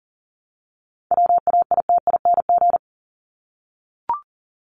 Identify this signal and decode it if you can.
Morse code, heard as a keyed tone, about 40 wpm, 700 Hz. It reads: WAITING
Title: ING Suffix - Top 100 Words with Base Words 40wpm